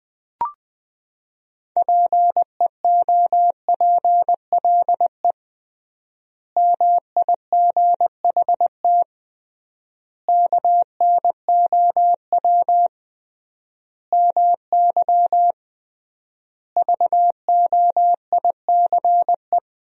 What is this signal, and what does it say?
Morse code, heard as a keyed tone, about 20 wpm, 700 Hz. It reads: PEOPLE MIGHT KNOW MY VOICE
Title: 3x Word-Spacing - Sentences from Top 500 Words - 20wpm